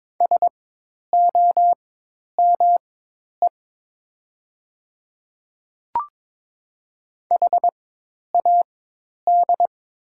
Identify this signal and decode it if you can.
Morse code, heard as a keyed tone, about 22 wpm, spaced as for 10 wpm, 700 Hz. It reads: SOME HAD
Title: 4x Character Spacing - Top 100 Words - 22wpm